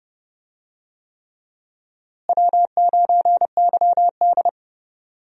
Morse code, heard as a keyed tone, about 30 wpm, 700 Hz. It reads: W9YD
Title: USA Call Signs - 4 Characters - Repeated 3x - 30wpm